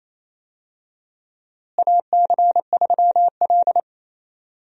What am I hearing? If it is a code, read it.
Morse code, heard as a keyed tone, about 28 wpm, 700 Hz. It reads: AC3L
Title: USA Call Signs - 4 Characters - Repeated 3x - 28wpm